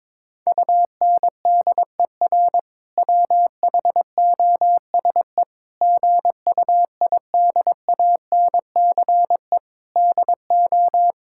Morse code, heard as a keyed tone, about 22 wpm, 700 Hz. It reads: UNDER WHOSE GUIDANCE DO
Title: Sentences from Top 1600 Words 22wpm